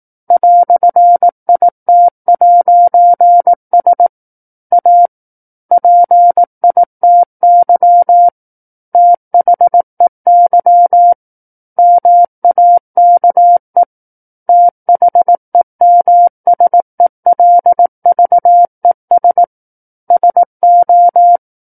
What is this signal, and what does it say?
Morse code, heard as a keyed tone, about 18 wpm, 700 Hz. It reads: "IT'S A PITY THEY MAKE THEMSELVES SO
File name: 18 WPM